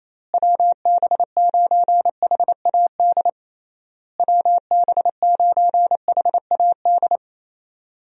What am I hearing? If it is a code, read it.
Morse code, heard as a keyed tone, about 27 wpm, 700 Hz. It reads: WB9HAD WB9HAD
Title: USA Call Signs - 6 Characters - Repeated 3x - 28wpm